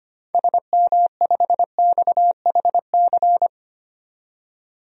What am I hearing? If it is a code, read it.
Morse code, heard as a keyed tone, about 25 wpm, 700 Hz. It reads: SM5XHC
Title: DXCC-100 Call Signs - 6 Characters - 25wpm